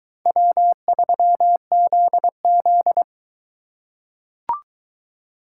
Morse code, heard as a keyed tone, about 22 wpm, 700 Hz. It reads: W3ZZ